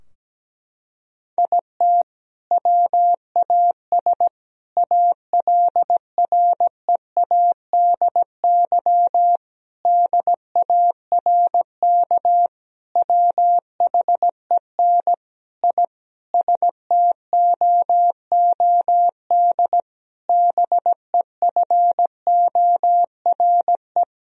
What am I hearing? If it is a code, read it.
Morse code, heard as a keyed tone, about 17 wpm, 700 Hz. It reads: IT WAS ALREADY DARK WHEN I STOOD BEFORE
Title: Sentences from Top 1000 Words - Review Part 2 - 17wpm